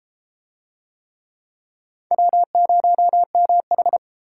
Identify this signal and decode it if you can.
Morse code, heard as a keyed tone, about 33 wpm, 700 Hz. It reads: W0MH